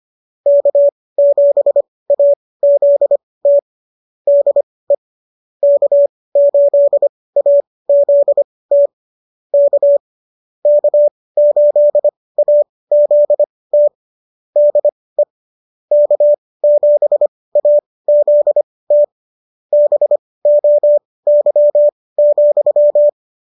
Morse code, heard as a keyed tone, about 25 wpm, spaced as for 18 wpm, 590 Hz. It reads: K7AZT DE K8AZT K K8AZT DE K7AZT BOY,